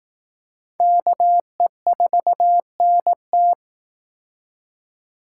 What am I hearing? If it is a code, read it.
Morse code, heard as a keyed tone, about 18 wpm, 700 Hz. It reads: KE4NT
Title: Speed Racing - USA 2x2 Call Signs 15wpm